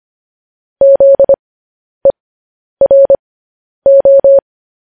Morse code, heard as a keyed tone, about 25 wpm, spaced as for 10 wpm, 550 Hz. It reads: ZERO
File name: podcast / News headlines-1x-25x10